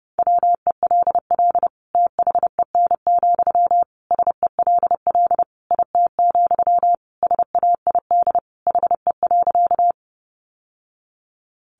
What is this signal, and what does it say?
Morse code, heard as a keyed tone, about 30 wpm, 700 Hz. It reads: WELL THEN, SELL IT, SAID HE.
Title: Sentences from Top 1100 Words 30wpm